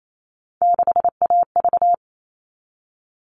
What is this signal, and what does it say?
Morse code, heard as a keyed tone, about 28 wpm, 700 Hz. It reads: 6AV
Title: All Cut Numbers - 3-Digit 28wpm